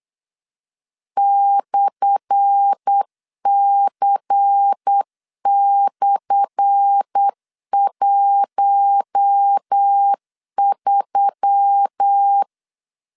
Morse code, heard as a keyed tone, about 8 wpm, 800 Hz. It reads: /C/13
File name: LEZIONI / 10 SIGNS / 10A SLASH 8wpm 3452